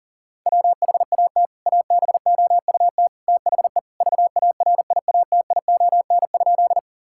Morse code, heard as a keyed tone, about 40 wpm, 700 Hz. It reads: WHAT ABOUT THE VARIATION?
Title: Sentences from Top 1600 Words 40wpm